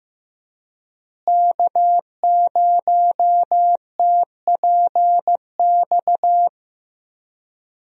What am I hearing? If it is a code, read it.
Morse code, heard as a keyed tone, about 15 wpm, 700 Hz. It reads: K0TPX
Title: USA Call Signs - 5 Characters - Repeated 2x - 15wpm